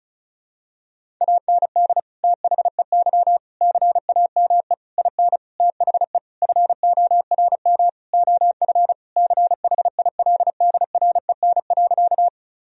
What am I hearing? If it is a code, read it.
Morse code, heard as a keyed tone, about 34 wpm, 700 Hz. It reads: AND THEY CAME IN THE FORM OF CHILDREN.
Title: Sentences from Top 1000 Words - Review Part 1 - 35wpm